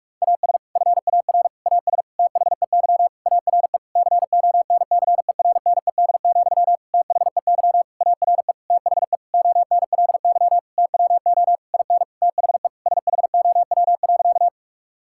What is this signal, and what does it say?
Morse code, heard as a keyed tone, about 46 wpm, 700 Hz. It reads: AS FAR AS THEY ARE CONCERNED, THEY ARE THE ONLY TWO IN THE SHOW.
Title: Sentences from Top 1000 Words - Review Part 6 - 45wpm